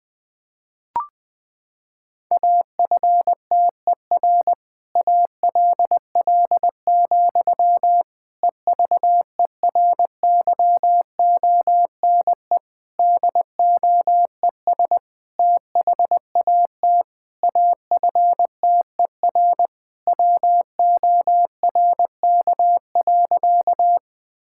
Morse code, heard as a keyed tone, about 21 wpm, 700 Hz. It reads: AFTER ALL, EVERYONE DOES THAT AFTER WORK.
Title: Sentences from Top 1000 Words - Review Part 5 - 20wpm